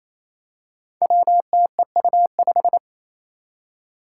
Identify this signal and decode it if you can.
Morse code, heard as a keyed tone, about 29 wpm, 700 Hz. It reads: WTEU5
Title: Five Challenging Letter-Number Combinations 28wpm